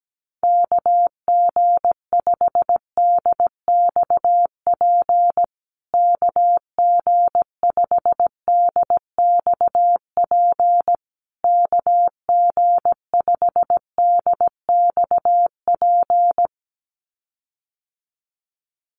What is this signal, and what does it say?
Morse code, heard as a keyed tone, about 17 wpm, 700 Hz. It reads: KG5DXP KG5DXP KG5DXP
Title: USA Call Signs Repeated 3x 17wpm